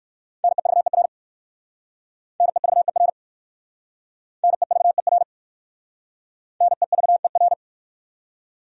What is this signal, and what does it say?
Morse code, heard as a keyed tone, about 56 wpm, 690 Hz. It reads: NEVER NEVER NEVER NEVER
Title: Speed-Racing Top 500 Words 45wpm